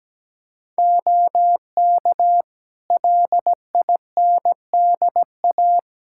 Morse code, heard as a keyed tone, about 17 wpm, 700 Hz. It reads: OK LINDA
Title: Practice QSO - Exchange 3 - More Info - 17wpm